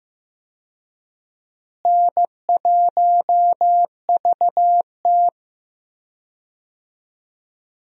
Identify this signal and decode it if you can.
Morse code, heard as a keyed tone, about 15 wpm, 700 Hz. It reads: N1VT
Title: USA Call Signs - 4 Characters - Repeated 2x - 15wpm